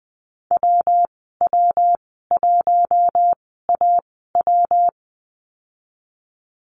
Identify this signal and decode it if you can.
Morse code, heard as a keyed tone, about 20 wpm, spaced as for 14 wpm, 700 Hz. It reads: WW1AW
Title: 128 Lesson 21g - Number 6 - Call Signs 20wpm